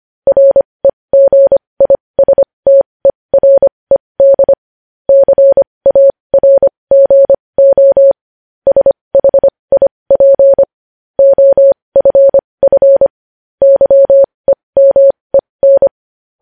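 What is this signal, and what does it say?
Morse code, heard as a keyed tone, about 25 wpm, spaced as for 20 wpm, 550 Hz. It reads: REGISTERED CARGO SHIP OFF YEMEN
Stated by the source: Morse code news headlines at 20 wpm.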